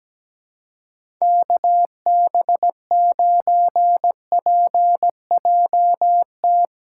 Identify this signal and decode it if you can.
Morse code, heard as a keyed tone, about 17 wpm, 700 Hz. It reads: KB9PJT
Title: USA Call Signs - 3x Character Spacing - 17wpm